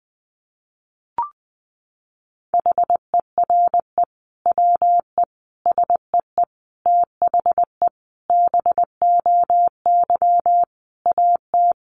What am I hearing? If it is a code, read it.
Morse code, heard as a keyed tone, about 20 wpm, 700 Hz. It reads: HERE WE SEE THE BOY AT
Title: Sentences from Top 300 Words 20wpm